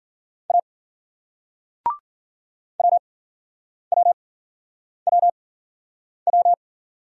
Morse code, heard as a keyed tone, about 50 wpm, spaced as for 9 wpm, 700 Hz. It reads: A WWWW
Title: Speed-Racing Single Letter-Number 40wpm